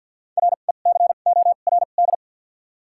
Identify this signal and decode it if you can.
Morse code, heard as a keyed tone, about 47 wpm, spaced as for 35 wpm, 700 Hz. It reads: RECORD
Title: 2x Character Spacing - Top 500 Words - 50wpm